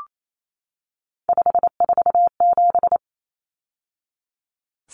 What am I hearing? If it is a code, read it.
Morse code, heard as a keyed tone, about 28 wpm, 700 Hz. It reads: 547
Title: Numbers - 3-Digit 28wpm